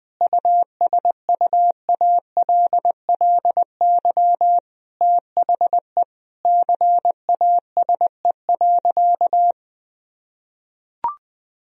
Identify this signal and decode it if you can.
Morse code, heard as a keyed tone, about 21 wpm, 700 Hz. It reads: USUALLY THE CASE.
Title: Sentences from Top 1000 Words - Review Part 3 - 20wpm